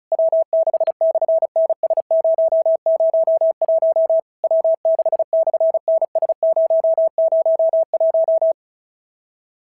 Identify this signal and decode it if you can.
Morse code, heard as a keyed tone, about 36 wpm, 650 Hz. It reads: W6/NS001 W6/NS001
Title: SOTA QSO Practice 35wpm